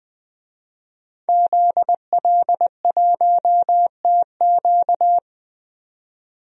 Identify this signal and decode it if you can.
Morse code, heard as a keyed tone, about 20 wpm, 700 Hz. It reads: ZL1TQ
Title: DXCC-100 Call Signs - 5 Characters - 20wpm